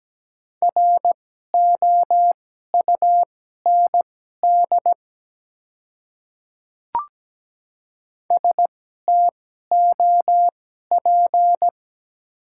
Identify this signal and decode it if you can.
Morse code, heard as a keyed tone, about 17 wpm, spaced as for 12 wpm, 700 Hz. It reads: ROUND STOP
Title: 2x Character Spacing - Top 300 Words - 17wpm